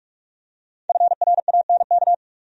Morse code, heard as a keyed tone, about 45 wpm, 700 Hz. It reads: FRANK
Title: USA Common Names - 45wpm